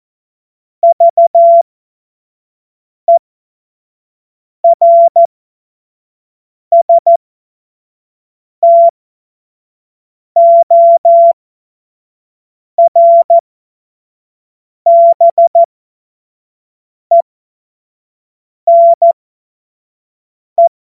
Morse code, heard as a keyed tone, about 14 wpm, spaced as for 5 wpm, 680 Hz. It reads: VERSTORBENE
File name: Witz-07-Oktober-2025-25-F70